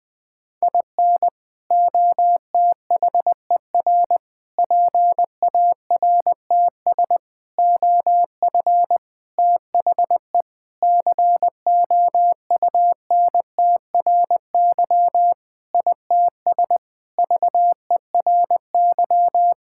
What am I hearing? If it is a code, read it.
Morse code, heard as a keyed tone, about 20 wpm, 700 Hz. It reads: IN OTHER PARTS OF THE COUNTRY ITS VERY
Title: Sentences with 5-Word Phrases - 20wpm